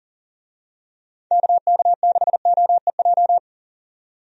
Morse code, heard as a keyed tone, about 40 wpm, 700 Hz. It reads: KK6OEJ
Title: USA Call Signs - 6 Characters - 40wpm